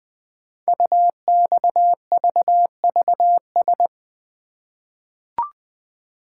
Morse code, heard as a keyed tone, about 20 wpm, 700 Hz. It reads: UXVVS
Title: Code Groups - 5 Letters - 20wpm